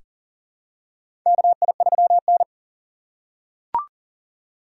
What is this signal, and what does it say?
Morse code, heard as a keyed tone, about 40 wpm, 700 Hz. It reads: KI3N
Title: USA Call Signs - 4 Characters - 6x Character Spacing - 40wpm